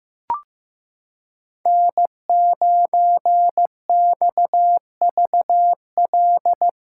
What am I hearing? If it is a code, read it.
Morse code, heard as a keyed tone, about 16 wpm, 700 Hz. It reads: N9XVL
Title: USA Callsign DE Callsign 15wpm